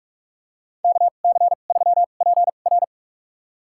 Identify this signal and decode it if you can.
Morse code, heard as a keyed tone, about 43 wpm, spaced as for 32 wpm, 700 Hz. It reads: KC3PR
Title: USA Call Signs - 5 Characters - 2x Character Spacing - 45wpm